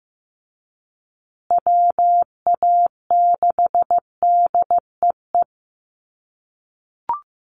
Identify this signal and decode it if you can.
Morse code, heard as a keyed tone, about 15 wpm, 700 Hz. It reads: WA6DEE
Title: USA Call Signs Repeated 2x 15wpm